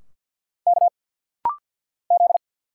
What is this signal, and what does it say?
Morse code, heard as a keyed tone, about 50 wpm, 700 Hz. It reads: K Z
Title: Single Letters - Rapid-Fire 50wpm